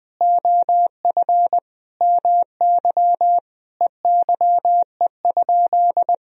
Morse code, heard as a keyed tone, about 20 wpm, 700 Hz. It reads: OF MY EYE?
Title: Sentences from Top 500 Words and Abbreviations 20wpm